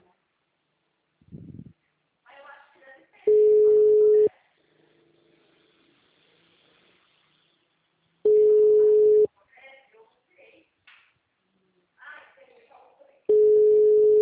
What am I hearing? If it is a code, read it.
Ringback tone (Europe, 425 Hz), with speech.